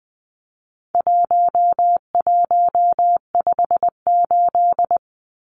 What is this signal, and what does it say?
Morse code, heard as a keyed tone, about 20 wpm, 700 Hz. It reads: 1158
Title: Numbers - 4-Digit 20wpm